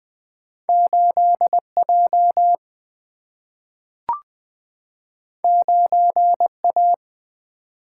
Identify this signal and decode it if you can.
Morse code, heard as a keyed tone, about 20 wpm, 700 Hz. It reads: 8 J 9 A
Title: Single-Letter Single-Number 20wpm